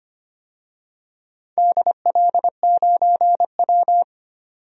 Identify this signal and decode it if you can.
Morse code, heard as a keyed tone, about 25 wpm, 700 Hz. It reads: DL9W
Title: DXCC-100 Call Signs - 4 Characters - 25wpm